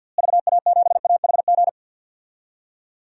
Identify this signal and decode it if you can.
Morse code, heard as a keyed tone, about 52 wpm, 700 Hz. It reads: VA7AHG
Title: DXCC-100 Call Signs - 6 Characters - Repeated 2x - 50wpm